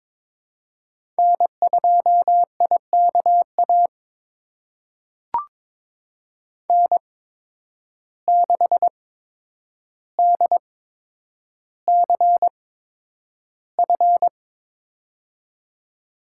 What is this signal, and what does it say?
Morse code, heard as a keyed tone, about 22 wpm, spaced as for 9 wpm, 700 Hz. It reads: N2IKA N6DCF
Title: USA Call Signs - 5 Characters - 8x Character Spacing - 22wpm